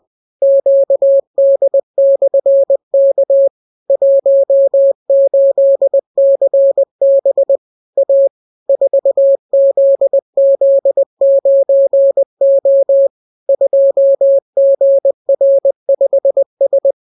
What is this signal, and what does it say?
Morse code, heard as a keyed tone, about 20 wpm, 550 Hz. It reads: QD/K 18CB A 4ZZ9O 2GR5S